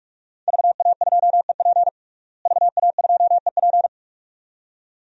Morse code, heard as a keyed tone, about 48 wpm, 700 Hz. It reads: VA2EP VA2EP
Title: DXCC-100 Call Signs - 5 Characters - Repeated 2x - 45wpm